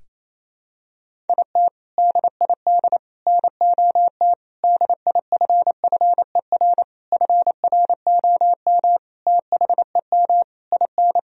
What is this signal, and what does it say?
Morse code, heard as a keyed tone, about 28 wpm, 700 Hz. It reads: IT DID NOT DIFFER FROM THEM IN
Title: Sentences from Top 1000 Words - Review Part 1 - 28wpm